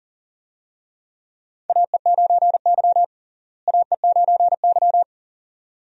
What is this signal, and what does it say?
Morse code, heard as a keyed tone, about 37 wpm, 700 Hz. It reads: AE9Y AE9Y
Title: USA Call Signs - 4 Characters - Repeated 2x - 40wpm